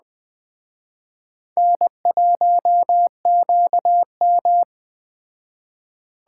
Morse code, heard as a keyed tone, about 20 wpm, 700 Hz. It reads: N1QM
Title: Speed Racing - USA 1x2 Call Signs 17wpm